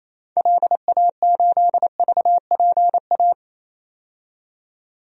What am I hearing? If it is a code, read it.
Morse code, heard as a keyed tone, about 28 wpm, 700 Hz. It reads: LA8VPA
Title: DXCC-100 Call Signs - 6 Characters - 28wpm